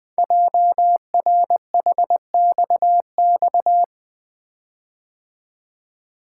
Morse code, heard as a keyed tone, about 20 wpm, 700 Hz. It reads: JRHXX
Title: Code Groups - 5 Letters - 20wpm